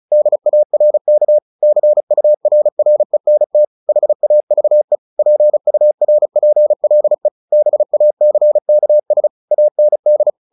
Morse code, heard as a keyed tone, about 34 wpm, 600 Hz. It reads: DARK CURRENT HAVE PURPLE BACKS AND
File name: the old man and the sea 2 35wpm